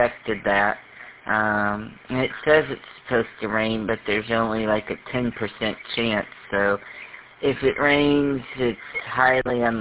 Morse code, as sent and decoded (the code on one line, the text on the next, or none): none